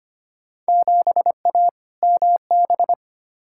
Morse code, heard as a keyed tone, about 25 wpm, 700 Hz. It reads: 7A MB
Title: Field Day Exchanges Encore - 25wpm